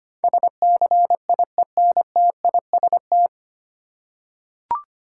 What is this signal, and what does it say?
Morse code, heard as a keyed tone, about 25 wpm, 700 Hz. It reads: SCIENTIST
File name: Nine Letter Words - 25wpm